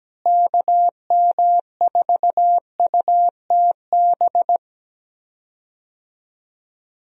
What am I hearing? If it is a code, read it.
Morse code, heard as a keyed tone, about 17 wpm, 700 Hz. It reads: KM4UTB
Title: USA Call Signs - 6 Characters - Repeated 2x - 17wpm